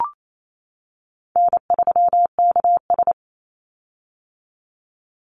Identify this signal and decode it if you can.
Morse code, heard as a keyed tone, about 28 wpm, 700 Hz. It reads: N3KS
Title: Easy Call Signs - 28wpm